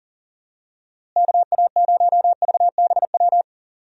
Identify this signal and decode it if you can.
Morse code, heard as a keyed tone, about 40 wpm, 700 Hz. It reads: KA0VBW
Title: USA Call Signs - 6 Characters - Repeated 2x - 40wpm